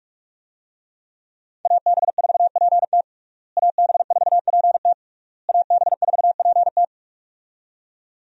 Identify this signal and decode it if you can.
Morse code, heard as a keyed tone, about 46 wpm, 700 Hz. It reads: AB4PT AB4PT AB4PT
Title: USA Call Signs - 5 Characters - Repeated 3x - 45wpm